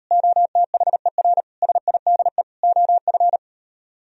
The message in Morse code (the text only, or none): OTHER SIDE OF